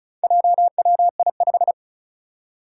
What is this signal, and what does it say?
Morse code, heard as a keyed tone, about 36 wpm, 700 Hz. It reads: JWI5
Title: Four Challenging Letter-Number Combinations 35wpm